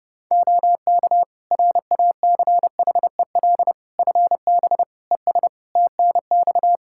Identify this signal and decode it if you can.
Morse code, heard as a keyed tone, about 30 wpm, 700 Hz. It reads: OK RACHEL FB ES TNX
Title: Practice QSO - Exchange 2 - Info - 30wpm